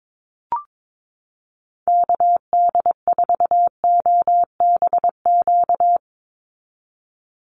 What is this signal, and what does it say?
Morse code, heard as a keyed tone, about 22 wpm, 700 Hz. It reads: KD4OBQ
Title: USA Call Signs 22wpm